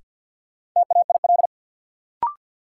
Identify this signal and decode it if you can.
Morse code, heard as a keyed tone, about 50 wpm, 700 Hz. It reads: TAIL
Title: Sentences from Top 1000 Words - Review Part 3 - New Word Drill - 50wpm